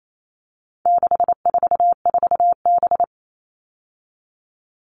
Morse code, heard as a keyed tone, about 28 wpm, 700 Hz. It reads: 644B
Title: All Cut Numbers - 4-Digit 28wpm